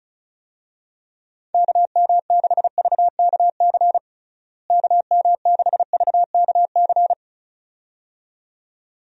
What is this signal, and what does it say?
Morse code, heard as a keyed tone, about 36 wpm, 700 Hz. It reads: KM6VKC KM6VKC
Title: USA Call Signs - 6 Characters - Repeated 2x - 35wpm